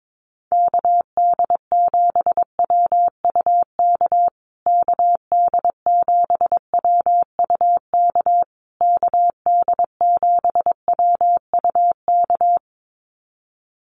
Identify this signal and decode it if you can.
Morse code, heard as a keyed tone, about 22 wpm, 700 Hz. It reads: KD7WUK KD7WUK KD7WUK
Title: USA Call Signs Repeated 3x 22wpm